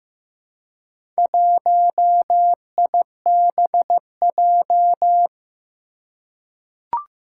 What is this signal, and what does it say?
Morse code, heard as a keyed tone, about 15 wpm, 700 Hz. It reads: 1IBJ
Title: Four Challenging Letter-Number Combinations 15wpm